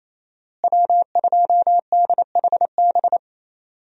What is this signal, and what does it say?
Morse code, heard as a keyed tone, about 28 wpm, 700 Hz. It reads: W2DHB
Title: USA Call Signs - 5 Characters - Repeated 3x - 28wpm